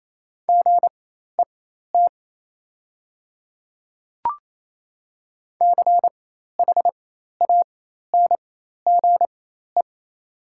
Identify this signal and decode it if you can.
Morse code, heard as a keyed tone, about 28 wpm, spaced as for 13 wpm, 700 Hz. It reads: GET CHANGE
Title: 4x Character Spacing - Top 200 Words - 28wpm